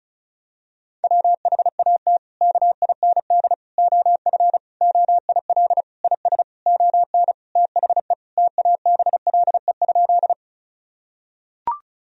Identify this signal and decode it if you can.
Morse code, heard as a keyed tone, about 36 wpm, 700 Hz. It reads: WHAT KIND OF OIL IS ON THE TABLE?
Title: Sentences from Top 1000 Words - Review Part 4 - 35wpm